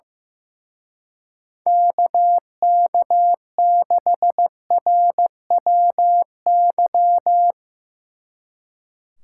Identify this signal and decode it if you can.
Morse code, heard as a keyed tone, about 16 wpm, 700 Hz. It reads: KK6RWY
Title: Speed Racing - USA Call Signs 15wpm